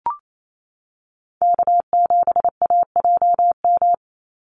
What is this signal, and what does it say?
Morse code, heard as a keyed tone, about 28 wpm, 700 Hz. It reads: K7AJM
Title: Call Signs - 28wpm